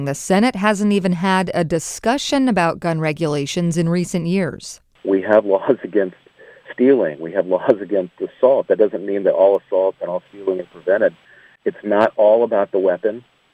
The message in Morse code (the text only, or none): none